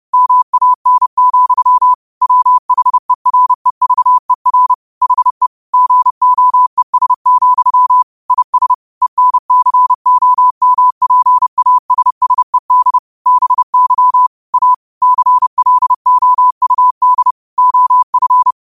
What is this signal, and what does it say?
Morse code, heard as a keyed tone, about 30 wpm, 1000 Hz. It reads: MAN, WHEREVER HE GOES, IS ENCOMPASSED BY A CLOUD OF
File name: qotdfiles / 30wpm